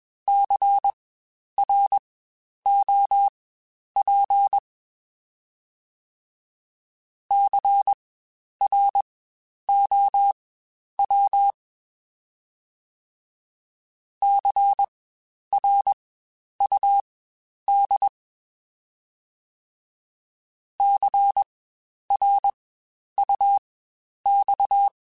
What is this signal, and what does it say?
Morse code, heard as a keyed tone, about 21 wpm, spaced as for 10 wpm, 800 Hz. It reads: CROP CROW CRUD CRUX
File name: Spaced Alphas / Spaced Alphas mp3s / Cs sp @20